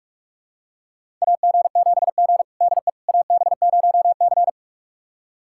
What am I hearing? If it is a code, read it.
Morse code, heard as a keyed tone, about 46 wpm, 700 Hz. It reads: AG7G DE AB0C
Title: USA Callsign DE Callsign - 4 Characters - 45wpm